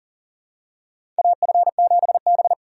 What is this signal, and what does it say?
Morse code, heard as a keyed tone, about 40 wpm, 700 Hz. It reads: AF7B
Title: USA Call Signs - 4 Characters - 3x Character Spacing - 40wpm